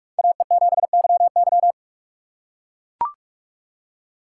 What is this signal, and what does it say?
Morse code, heard as a keyed tone, about 43 wpm, 700 Hz. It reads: AE7YY
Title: USA Call Signs - 5 Characters - 2x Character Spacing - 45wpm